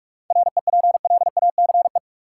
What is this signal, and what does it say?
Morse code, heard as a keyed tone, about 46 wpm, 700 Hz. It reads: REPLACE
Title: Sentences from Top 1400 Words - New Word Drill - 45wpm